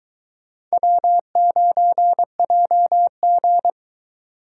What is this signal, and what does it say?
Morse code, heard as a keyed tone, about 23 wpm, 700 Hz. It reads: W9JG